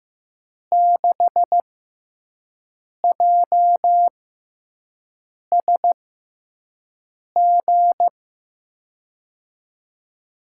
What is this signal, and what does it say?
Morse code, heard as a keyed tone, about 15 wpm, spaced as for 5 wpm, 700 Hz. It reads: 6JSG
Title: USA Call Signs - 5 Characters - 6x Character Spacing - 15wpm